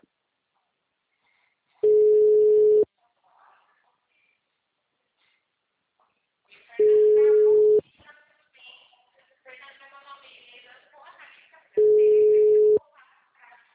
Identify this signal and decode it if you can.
Ringback tone (Europe, 425 Hz), with speech.